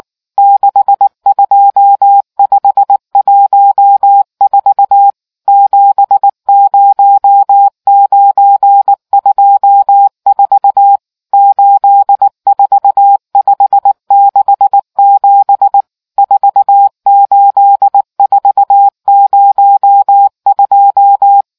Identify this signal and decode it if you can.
Morse code, heard as a keyed tone, about 19 wpm, 800 Hz. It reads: 62514 70924 84567 48402